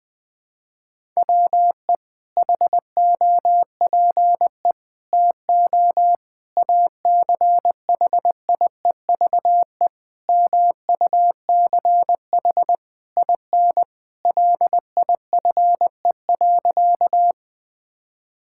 Morse code, heard as a keyed tone, about 20 wpm, 700 Hz. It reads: WE HOPE TO ACHIEVE MUCH IN LIFE.
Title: Sentences from Top 1000 Words - Review Part 6 - 20wpm